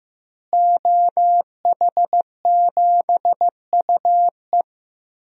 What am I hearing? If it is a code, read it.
Morse code, heard as a keyed tone, about 15 wpm, 700 Hz. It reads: OH7UE
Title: DXCC-100 Call Signs - 5 Characters - Repeated 3x - 15wpm